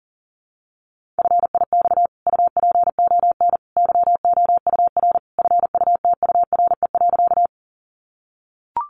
Morse code, heard as a keyed tone, about 40 wpm, 700 Hz. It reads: FIX UPON YOUR FUTURE.
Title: Sentences from Top 1400 Words 40wpm